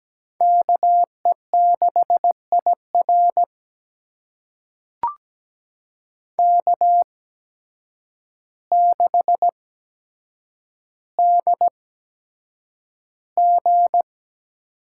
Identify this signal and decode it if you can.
Morse code, heard as a keyed tone, about 17 wpm, spaced as for 8 wpm, 700 Hz. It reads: KE6IR K6DG
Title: USA Call Signs - 5 Characters - 8x Character Spacing - 17wpm